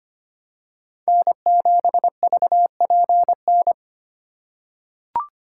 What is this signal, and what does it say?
Morse code, heard as a keyed tone, about 25 wpm, 700 Hz. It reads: N7VPN
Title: USA Call Signs - 5 Characters - 3x Character Spacing - 25wpm